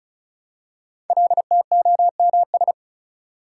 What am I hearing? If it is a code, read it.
Morse code, heard as a keyed tone, about 36 wpm, 700 Hz. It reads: LTOMS
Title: Code Groups - 5 Letter:Numbers - 35wpm